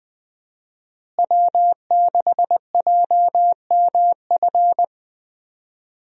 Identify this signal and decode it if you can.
Morse code, heard as a keyed tone, about 21 wpm, 700 Hz. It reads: W6JMF
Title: USA Call Signs - 5 Characters - Repeated 3x - 20wpm